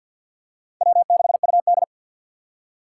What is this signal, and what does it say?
Morse code, heard as a keyed tone, about 53 wpm, 690 Hz. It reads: W6UD
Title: USA Call Signs - 4 Characters - 6x Character Spacing - 50wpm